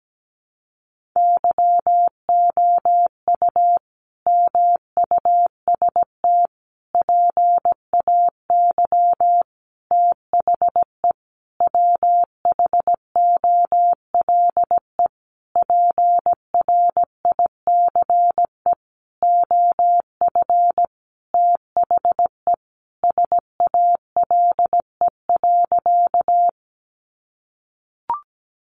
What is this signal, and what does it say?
Morse code, heard as a keyed tone, about 17 wpm, 700 Hz. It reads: YOU MUST PAY THE WHOLE PRICE OF THE SALE.
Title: Sentences from Top 1400 Words 17wpm